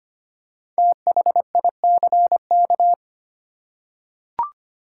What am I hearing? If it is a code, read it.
Morse code, heard as a keyed tone, about 25 wpm, 700 Hz. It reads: THICK
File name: 5 Letter Words - First 3 Letters Given - 25wpm